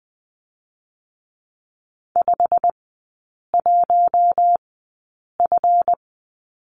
Morse code, heard as a keyed tone, about 20 wpm, spaced as for 8 wpm, 700 Hz. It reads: 51F
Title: 138 Lesson 23e - Letter f - Three Characters 20wpm